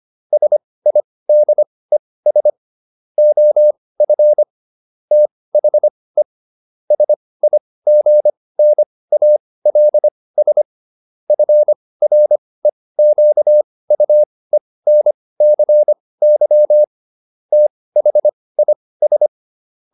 Morse code, heard as a keyed tone, about 25 wpm, spaced as for 18 wpm, 600 Hz. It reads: SIDES OF THE SIGNALS FREQUENCY THIS